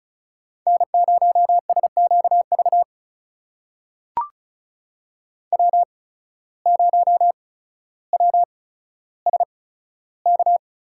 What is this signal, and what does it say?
Morse code, heard as a keyed tone, about 36 wpm, spaced as for 15 wpm, 700 Hz. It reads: N0SQV W0WSK
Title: USA Call Signs - 5 Characters - 8x Character Spacing - 35wpm